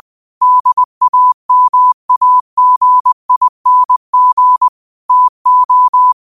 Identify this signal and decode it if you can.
Morse code, heard as a keyed tone, about 20 wpm, 1000 Hz. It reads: DAMAGING TO
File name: qotdfiles / 20wpm